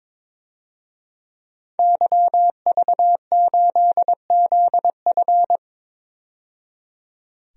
Morse code, heard as a keyed tone, about 22 wpm, 700 Hz. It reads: YV8ZF
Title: DXCC-100 Call Signs - 5 Characters - 22wpm